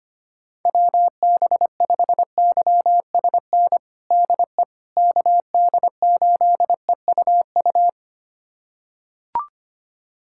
Morse code, heard as a keyed tone, about 25 wpm, 700 Hz. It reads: WB5YSN DE KD8EUU
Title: USA Callsign DE Callsign - 6 Characters - 25wpm